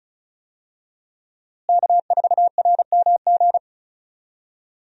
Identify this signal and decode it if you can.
Morse code, heard as a keyed tone, about 36 wpm, 700 Hz. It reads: K4RMG
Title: USA Call Signs - 5 Characters - Repeated 3x - 35wpm